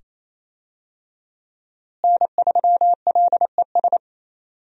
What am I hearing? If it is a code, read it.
Morse code, heard as a keyed tone, about 28 wpm, 700 Hz. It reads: N3LES
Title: USA Call Signs - 5 Characters - Repeated 2x - 28wpm